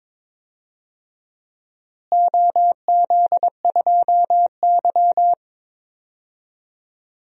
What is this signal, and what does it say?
Morse code, heard as a keyed tone, about 22 wpm, 700 Hz. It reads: OZ2Y
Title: DXCC-100 Call Signs - 4 Characters - Repeated 2x - 22wpm